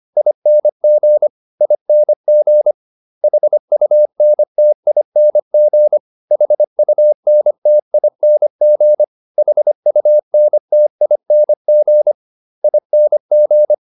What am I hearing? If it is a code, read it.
Morse code, heard as a keyed tone, about 25 wpm, 600 Hz. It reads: ING ING HUNTING HUNTING HUNTING ING